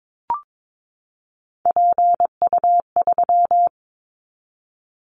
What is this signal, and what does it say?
Morse code, heard as a keyed tone, about 22 wpm, 700 Hz. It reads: PU3
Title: Callsign Prefixes for Top 50 DXCC - 22wpm